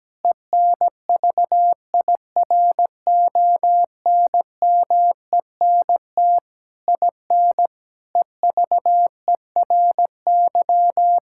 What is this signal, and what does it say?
Morse code, heard as a keyed tone, about 17 wpm, 700 Hz. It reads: ENVIRONMENT IN EVERY
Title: Sentences from Top 1000 Words - Review Part 5 - 17wpm